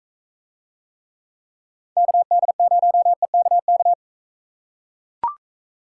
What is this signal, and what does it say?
Morse code, heard as a keyed tone, about 44 wpm, 700 Hz. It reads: KD0EKK